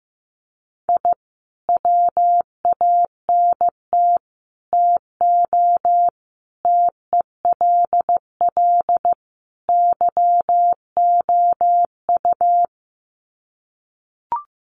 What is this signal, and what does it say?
Morse code, heard as a keyed tone, about 15 wpm, 700 Hz. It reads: I WANT TO TELL YOU
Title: Sets of 5 Words Encore 15wpm